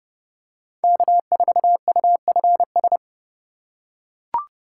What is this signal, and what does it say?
Morse code, heard as a keyed tone, about 31 wpm, 700 Hz. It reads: K4UFS